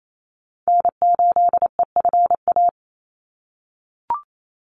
Morse code, heard as a keyed tone, about 28 wpm, 700 Hz. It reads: N8EFA
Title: USA Call Signs 28wpm